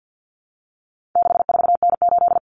Morse code, heard as a keyed tone, about 50 wpm, 700 Hz. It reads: 64N8
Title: Common Cut Numbers - 4-Digit 50wpm